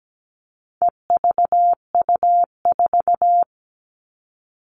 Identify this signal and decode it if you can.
Morse code, heard as a keyed tone, about 17 wpm, 700 Hz. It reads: EVU4
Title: All Cut Numbers - 4-Digit 17wpm